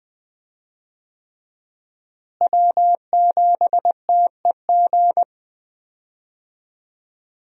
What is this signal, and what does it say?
Morse code, heard as a keyed tone, about 20 wpm, 700 Hz. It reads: W7TEG
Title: USA Call Signs - 5 Characters - Repeated 3x - 20wpm